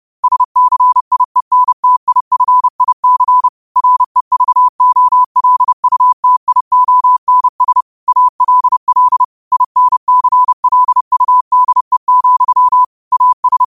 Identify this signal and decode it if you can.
Morse code, heard as a keyed tone, about 30 wpm, 1000 Hz. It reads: SCIENTIFIC REVOLUTIONS ALL INCLUDE, AS
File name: qotdfiles / 30wpm